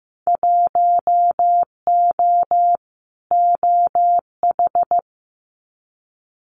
Morse code, heard as a keyed tone, about 15 wpm, 700 Hz. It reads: 1O OH
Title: Winter Field Day Exchanges 15wpm